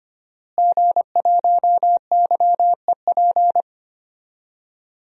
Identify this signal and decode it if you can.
Morse code, heard as a keyed tone, about 25 wpm, 700 Hz. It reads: G1YEP
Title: DXCC-100 Call Signs - 5 Characters - Repeated 2x - 25wpm